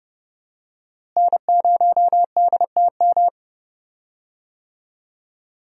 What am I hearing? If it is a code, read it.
Morse code, heard as a keyed tone, about 30 wpm, 700 Hz. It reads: N0DTM
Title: USA Call Signs - 5 Characters - Repeated 3x - 30wpm